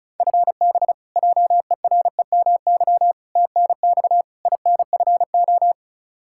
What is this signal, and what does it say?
Morse code, heard as a keyed tone, about 36 wpm, 700 Hz. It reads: FB JEREMY TNX INFO
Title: Practice QSO - Exchange 3 - More Info - 35wpm